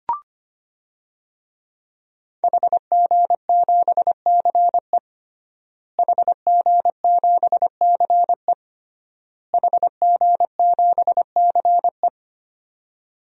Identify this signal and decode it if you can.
Morse code, heard as a keyed tone, about 25 wpm, 700 Hz. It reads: HG7CE HG7CE HG7CE
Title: DXCC-100 Call Signs - 5 Characters - Repeated 3x - 25wpm